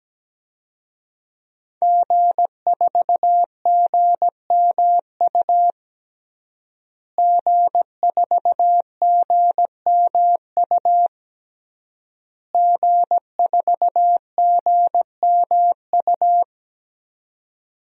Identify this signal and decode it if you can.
Morse code, heard as a keyed tone, about 17 wpm, 700 Hz. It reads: G4GMU G4GMU G4GMU
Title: DXCC-100 Call Signs - 5 Characters - Repeated 3x - 17wpm